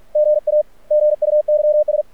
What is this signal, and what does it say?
Morse code, heard as a keyed tone, about 44 wpm, 600 Hz. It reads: GA GUYS
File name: archived / CRSnet